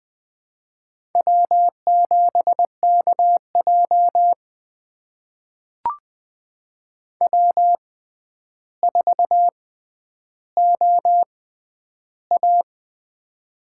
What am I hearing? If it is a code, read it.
Morse code, heard as a keyed tone, about 21 wpm, spaced as for 10 wpm, 700 Hz. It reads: W7KJ W4OA
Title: USA Call Signs - 4 Characters - 6x Character Spacing - 20wpm